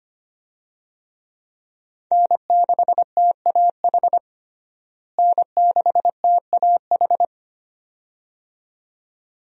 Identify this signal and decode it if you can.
Morse code, heard as a keyed tone, about 25 wpm, 700 Hz. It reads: N6TAH N6TAH
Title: USA Call Signs - 5 Characters - Repeated 2x - 25wpm